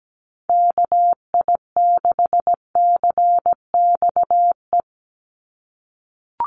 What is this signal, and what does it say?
Morse code, heard as a keyed tone, about 17 wpm, 700 Hz. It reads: KI6CXE
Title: USA Call Signs 17wpm